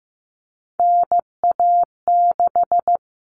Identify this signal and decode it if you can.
Morse code, heard as a keyed tone, about 15 wpm, 700 Hz. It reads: NA6
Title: Common Cut Numbers - 3-Digit 15wpm